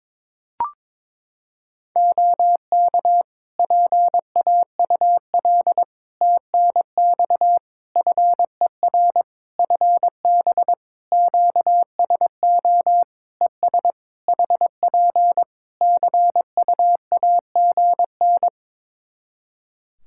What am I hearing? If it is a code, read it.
Morse code, heard as a keyed tone, about 22 wpm, 700 Hz. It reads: OK PAUL TNX FER FB QSO ES HP CUAGN
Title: K1USN SST QSO Practice 22wpm